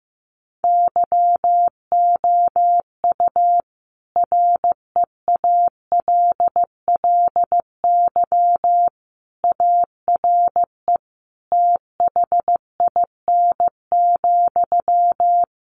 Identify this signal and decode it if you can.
Morse code, heard as a keyed tone, about 15 wpm, 700 Hz. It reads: YOU REALLY ARE THIN,
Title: Sentences from Top 1200 Words 15wpm